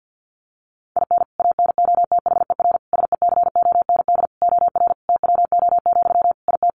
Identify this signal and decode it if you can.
Morse code, heard as a keyed tone, about 50 wpm, 700 Hz. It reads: IN ANOTHER SECOND OR TWO, IT
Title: Sentences from Top 1500 Words 50wpm